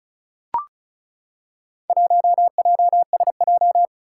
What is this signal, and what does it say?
Morse code, heard as a keyed tone, about 36 wpm, 700 Hz. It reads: 1JSJ
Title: Four Challenging Letter-Number Combinations 35wpm